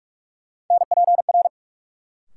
Morse code, heard as a keyed tone, about 46 wpm, 700 Hz. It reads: NPR
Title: Sets of 5 Words - New Word Drill - 45wpm